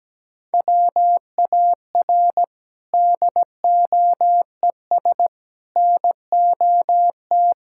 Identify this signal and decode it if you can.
Morse code, heard as a keyed tone, about 17 wpm, 700 Hz. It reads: WAR DOES NOT
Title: Sentences from Top 1000 Words - Review Part 5 - 17wpm